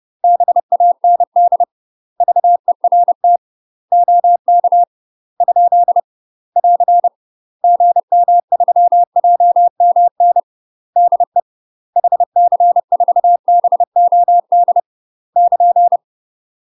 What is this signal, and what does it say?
Morse code, heard as a keyed tone, about 30 wpm, 700 Hz. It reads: BAND VERT OK ? <AR> GM3JMN DE HC4BOD <KN>
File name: UK MASTER QSOs TRAINING / 30WPM / UK MASTER QSOs 0051